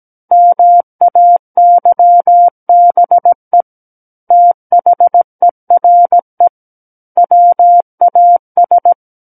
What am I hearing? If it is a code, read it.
Morse code, heard as a keyed tone, about 17 wpm, 700 Hz. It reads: MAYBE THERE WAS
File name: Book One / 08-Chapter 8 - 17 WPM